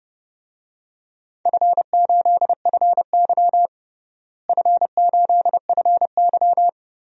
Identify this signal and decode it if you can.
Morse code, heard as a keyed tone, about 30 wpm, 700 Hz. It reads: F8FY F8FY
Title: DXCC-100 Call Signs - 4 Characters - Repeated 3x - 30wpm